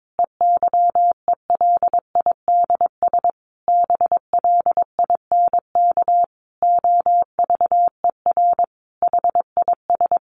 Morse code, heard as a keyed tone, about 22 wpm, 700 Hz. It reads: EYELIDS BLINK OVER HIS
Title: A Sound Of Thunder 22wpm